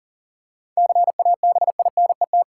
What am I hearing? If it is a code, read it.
Morse code, heard as a keyed tone, about 40 wpm, 700 Hz. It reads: CABINET
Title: Seven Letter Words - 40wpm